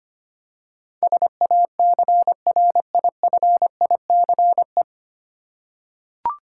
Morse code, heard as a keyed tone, about 25 wpm, 700 Hz. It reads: SACRIFICE
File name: Nine Letter Words - 25wpm